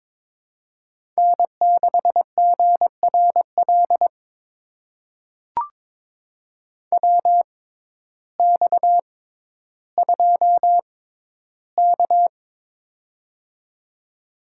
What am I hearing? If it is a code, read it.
Morse code, heard as a keyed tone, about 22 wpm, spaced as for 12 wpm, 700 Hz. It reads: N6GRL WX2K
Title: USA Call Signs - 6x Character Spacing - 22wpm